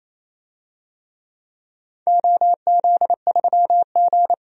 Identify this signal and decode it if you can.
Morse code, heard as a keyed tone, about 28 wpm, 700 Hz. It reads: OZ3G
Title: DXCC-100 Call Signs - 4 Characters - 28wpm